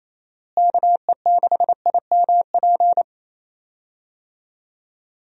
Morse code, heard as a keyed tone, about 28 wpm, 700 Hz. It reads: KE6IMP
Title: USA Call Signs - 6 Characters - Repeated 3x - 28wpm